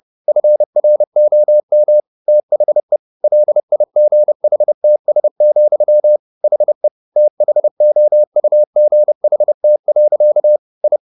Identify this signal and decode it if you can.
Morse code, heard as a keyed tone, about 30 wpm, 600 Hz. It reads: FROM THE LIGHTS, HE THOUGHT. I